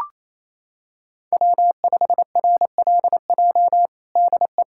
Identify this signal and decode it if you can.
Morse code, heard as a keyed tone, about 27 wpm, 700 Hz. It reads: W5RLJ DE
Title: USA Callsign DE Callsign - 5 Characters - 28wpm